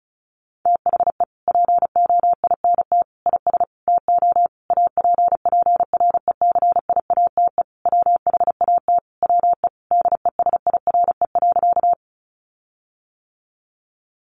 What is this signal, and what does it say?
Morse code, heard as a keyed tone, about 34 wpm, 700 Hz. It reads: THE POINT IS TO APPRECIATE WHAT WE DESIRE.
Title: Sentences from Top 1300 Words 35wpm